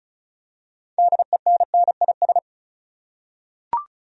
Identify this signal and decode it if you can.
Morse code, heard as a keyed tone, about 36 wpm, 700 Hz. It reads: DENNIS
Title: USA Common Names - 35wpm